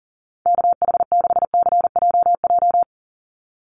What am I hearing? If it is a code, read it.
Morse code, heard as a keyed tone, about 40 wpm, 700 Hz. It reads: KH6CJJ
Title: Call Signs Repeated 2x - 40wpm